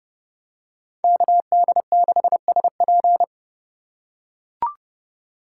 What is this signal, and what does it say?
Morse code, heard as a keyed tone, about 28 wpm, 700 Hz. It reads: KD6SP